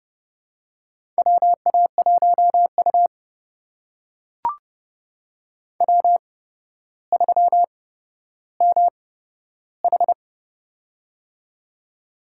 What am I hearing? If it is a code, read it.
Morse code, heard as a keyed tone, about 30 wpm, spaced as for 13 wpm, 700 Hz. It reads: WA1U W3MH
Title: USA Call Signs - 8x Character Spacing - 30wpm